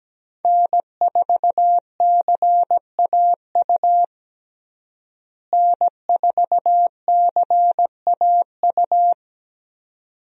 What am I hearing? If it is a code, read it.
Morse code, heard as a keyed tone, about 17 wpm, 700 Hz. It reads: N4CAU N4CAU
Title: USA Call Signs - 5 Characters - Repeated 2x - 17wpm